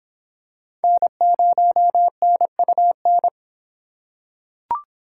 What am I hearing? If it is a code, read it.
Morse code, heard as a keyed tone, about 26 wpm, 700 Hz. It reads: N0NUN